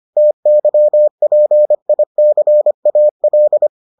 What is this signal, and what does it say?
Morse code, heard as a keyed tone, about 25 wpm, 600 Hz. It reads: TYPICAL